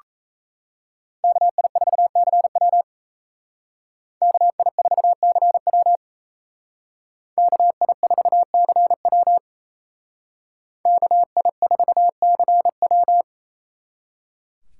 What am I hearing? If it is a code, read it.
Morse code, heard as a keyed tone, about 34 wpm, 700 Hz. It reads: KI4CW KI4CW KI4CW KI4CW
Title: Speed Racing - USA Call Signs 28wpm